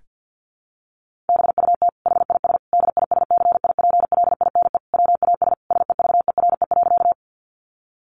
Morse code, heard as a keyed tone, about 50 wpm, 700 Hz. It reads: BUT HIS DISCIPLINE WAS SEVERE.
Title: Sentences from Top 1400 Words 50wpm